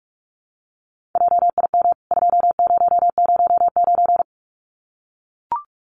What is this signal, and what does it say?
Morse code, heard as a keyed tone, about 44 wpm, 700 Hz. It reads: JIM 2009
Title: CWT Exchange 45wpm